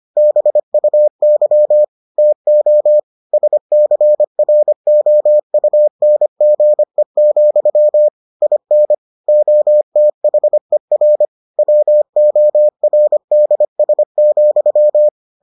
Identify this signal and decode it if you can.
Morse code, heard as a keyed tone, about 25 wpm, 600 Hz. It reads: BUY TO SCROUNGE, IN OTHER WORDS,